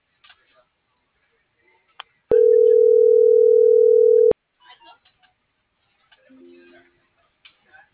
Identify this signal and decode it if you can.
Ringback tone (North America), 440 + 480 Hz.